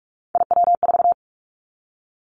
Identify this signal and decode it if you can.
Morse code, heard as a keyed tone, about 46 wpm, 700 Hz. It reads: IW4
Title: Callsign Prefixes for Top 50 DXCC - 45wpm